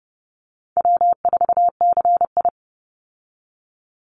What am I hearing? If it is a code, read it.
Morse code, heard as a keyed tone, about 30 wpm, 700 Hz. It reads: W4CI
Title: Easy Call Signs Repeated 3x - 30wpm